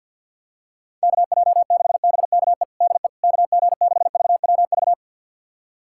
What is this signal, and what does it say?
Morse code, heard as a keyed tone, about 52 wpm, 700 Hz. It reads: KJ6BKE DE KG6VWV
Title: USA Callsign DE Callsign - 6 Characters - 50wpm